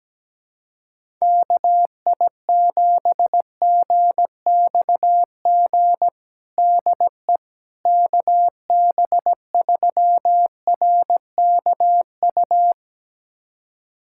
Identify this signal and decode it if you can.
Morse code, heard as a keyed tone, about 17 wpm, 700 Hz. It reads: KI7GXG DE KB3RKU
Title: USA Callsign DE Callsign 17wpm